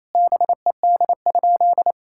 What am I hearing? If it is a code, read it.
Morse code, heard as a keyed tone, about 28 wpm, 700 Hz. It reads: BED?
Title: Sentences from Top 1000 Words - Review Part 6 - 28wpm